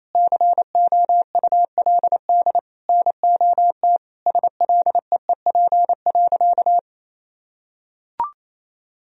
Morse code, heard as a keyed tone, about 28 wpm, 700 Hz. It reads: COULD NOT SLEEP.
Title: Sentences from Top 1000 Words - Review Part 3 - 28wpm